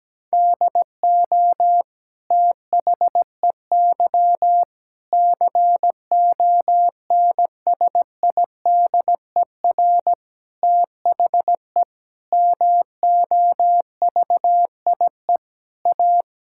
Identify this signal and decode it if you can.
Morse code, heard as a keyed tone, about 17 wpm, 700 Hz. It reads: DO THEY CONSIDER THE MOVIE A
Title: Sentences from Top 1000 Words - Review Part 6 - 17wpm